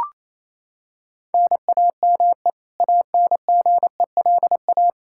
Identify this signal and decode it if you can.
Morse code, heard as a keyed tone, about 28 wpm, 700 Hz. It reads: NAME ANGELA
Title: Practice QSO - Exchange 1 - Beginning - 28wpm